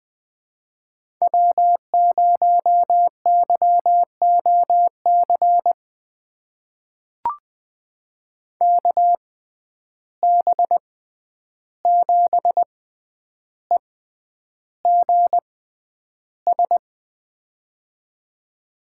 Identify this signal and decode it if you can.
Morse code, heard as a keyed tone, about 20 wpm, spaced as for 10 wpm, 700 Hz. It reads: W0YOC KB7EGS
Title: USA Call Signs - 6x Character Spacing - 20wpm